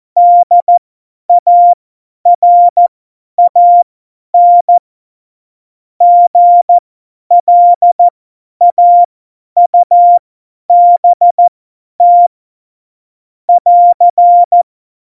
Morse code, heard as a keyed tone, about 14 wpm, spaced as for 10 wpm, 700 Hz. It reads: DARAN GLAUBT <AR>
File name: Morsekurs / Witze / Witz 26. Januar 2021, 50 F70